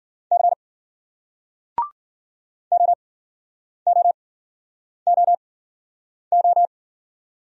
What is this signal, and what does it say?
Morse code, heard as a keyed tone, about 56 wpm, spaced as for 9 wpm, 700 Hz. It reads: C OOOO